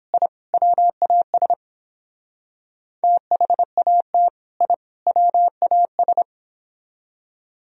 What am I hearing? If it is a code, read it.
Morse code, heard as a keyed tone, about 29 wpm, 700 Hz. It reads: I WAS THAT I WAS